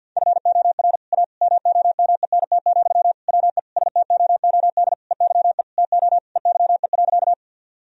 Morse code, heard as a keyed tone, about 50 wpm, 700 Hz. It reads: FOR A MOMENT, WE STOOD EYE TO EYE.
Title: Sentences from Top 1000 Words - Review Part 2 - 50wpm